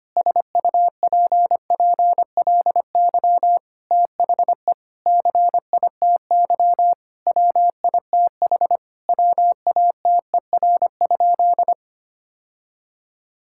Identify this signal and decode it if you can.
Morse code, heard as a keyed tone, about 25 wpm, 700 Hz. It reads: SUPPLY THE CITY WITH WATER?
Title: Sentences from Top 1000 Words - Review Part 5 - 25wpm